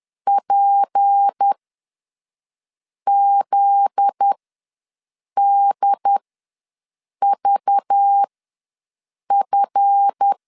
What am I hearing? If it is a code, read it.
Morse code, heard as a keyed tone, about 10 wpm, spaced as for 5 wpm, 800 Hz. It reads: PZDVF
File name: LEZIONI / 8 YJ45 / 8 YJ4 ws 10wpm 4020